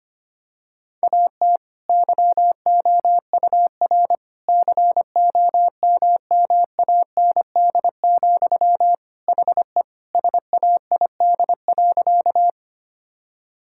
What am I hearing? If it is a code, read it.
Morse code, heard as a keyed tone, about 25 wpm, 700 Hz. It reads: AT YOUR COMMAND, HE SAID.
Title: Sentences from Top 1600 Words 25wpm